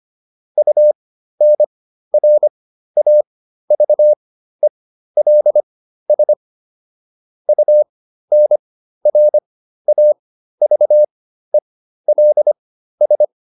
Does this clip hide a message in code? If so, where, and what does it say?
Yes: Morse code from 0.6 to 13.2 s, heard as a keyed tone, about 25 wpm, spaced as for 13 wpm, 600 Hz. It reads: UNRAVELS UNRAVELS